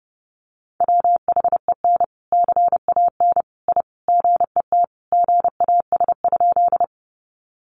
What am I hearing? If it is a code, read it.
Morse code, heard as a keyed tone, about 30 wpm, 700 Hz. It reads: WHEN CAN I GET GAS?
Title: Sentences from Top 1100 Words 30wpm